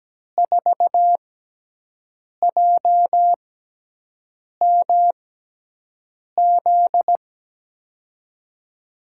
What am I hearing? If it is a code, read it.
Morse code, heard as a keyed tone, about 17 wpm, spaced as for 6 wpm, 700 Hz. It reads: 4JMZ
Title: USA Call Signs - 6 Characters - 6x Character Spacing - 17wpm